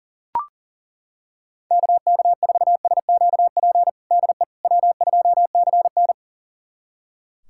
Morse code, heard as a keyed tone, about 40 wpm, 700 Hz. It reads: KK4SQP DE W2CN
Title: USA Callsign DE Callsign 40wpm